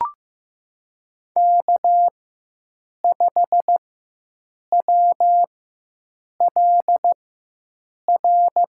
Morse code, heard as a keyed tone, about 15 wpm, spaced as for 7 wpm, 700 Hz. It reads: K5WLR
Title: USA Call Signs - 5 Characters - 4x Character Spacing - 15wpm